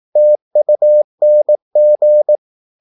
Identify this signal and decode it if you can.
Morse code, heard as a keyed tone, about 18 wpm, 600 Hz. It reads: TUNG
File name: Korte ord DK 18wpm